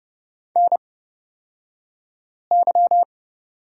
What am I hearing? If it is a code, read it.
Morse code, heard as a keyed tone, about 31 wpm, spaced as for 5 wpm, 700 Hz. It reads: NY
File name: Lesson 38d - Letter q - Two Characters 30wpm